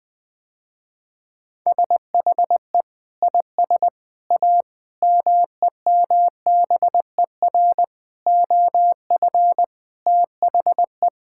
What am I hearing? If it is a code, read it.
Morse code, heard as a keyed tone, about 20 wpm, 700 Hz. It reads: SHE IS A MEMBER OF THE
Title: Sentences with 3-Word Phrases - Spoken Beginning - 20wpm